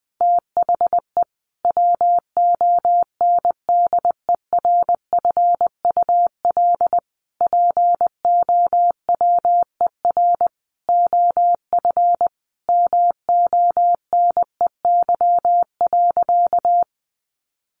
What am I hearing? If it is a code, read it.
Morse code, heard as a keyed tone, about 20 wpm, 700 Hz. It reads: THE WONDERFUL POWER OF MONEY.
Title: Sentences from Top 1100 Words 20wpm